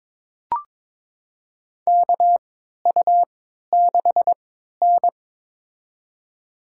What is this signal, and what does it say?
Morse code, heard as a keyed tone, about 22 wpm, spaced as for 12 wpm, 700 Hz. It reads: KU6N
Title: USA Call Signs - 4 Characters - 3x Character Spacing - 22wpm